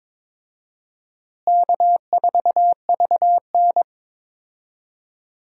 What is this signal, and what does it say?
Morse code, heard as a keyed tone, about 22 wpm, 700 Hz. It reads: K4VN
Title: USA Call Signs - 4 Characters - Repeated 2x - 22wpm